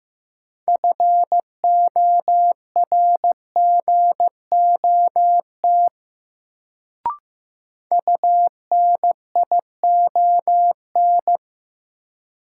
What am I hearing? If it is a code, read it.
Morse code, heard as a keyed tone, about 15 wpm, 700 Hz. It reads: FORGOT UNION
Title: Sentences from Top 1400 Words - New Word Drill Alt - 15wpm